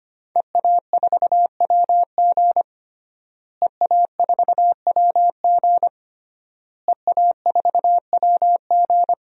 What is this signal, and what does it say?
Morse code, heard as a keyed tone, about 25 wpm, 700 Hz. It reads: EA4WG EA4WG EA4WG
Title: DXCC-100 Call Signs - 5 Characters - Repeated 3x - 25wpm